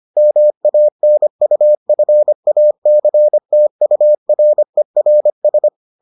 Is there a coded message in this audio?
Yes: Morse code from 0.2 to 5.7 s, heard as a keyed tone, about 25 wpm, 600 Hz. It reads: MANUFACTURERS